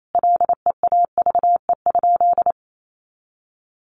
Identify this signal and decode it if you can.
Morse code, heard as a keyed tone, about 28 wpm, 700 Hz. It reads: LEAVE?
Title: Sentences from Top 1100 Words 28wpm